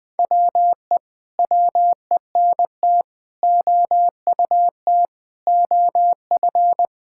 Morse code, heard as a keyed tone, about 20 wpm, 700 Hz. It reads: WE WENT OUT OF
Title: Sentences with 9-Word Phrases - 20wpm